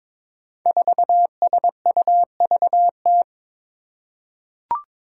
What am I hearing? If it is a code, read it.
Morse code, heard as a keyed tone, about 22 wpm, 700 Hz. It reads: 4SUVT
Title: Five Challenging Letter-Number Combinations 22wpm